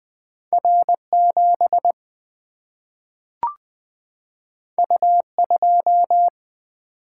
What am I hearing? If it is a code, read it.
Morse code, heard as a keyed tone, about 20 wpm, 700 Hz. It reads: R 7 U 2
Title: Single-Letter Single-Number 20wpm